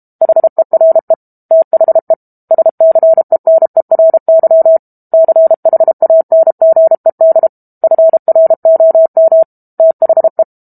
Morse code, heard as a keyed tone, about 32 wpm, 650 Hz. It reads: HERE THE SCENERY CHANGED FROM THE
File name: Book Two / 06-Chapter 6 - 32 WPM